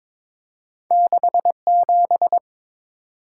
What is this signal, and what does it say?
Morse code, heard as a keyed tone, about 22 wpm, 690 Hz. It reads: 67
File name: practice / 22wpm / Two Letter-Number Combinations - 22wpm